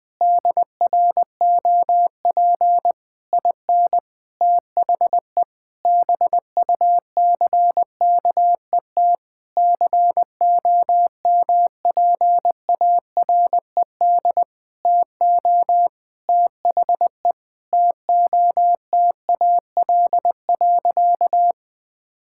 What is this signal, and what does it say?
Morse code, heard as a keyed tone, about 20 wpm, 700 Hz. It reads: DROP IN THE BUCKET COMPARED TO THE TOTAL.
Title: Sentences with 8-Word Phrases - Spoken Beginning - 20wpm